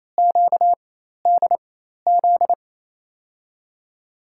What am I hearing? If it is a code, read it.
Morse code, heard as a keyed tone, about 28 wpm, spaced as for 13 wpm, 700 Hz. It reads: QDZ
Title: USA Call Signs - 6 Characters - 4x Character Spacing - 28wpm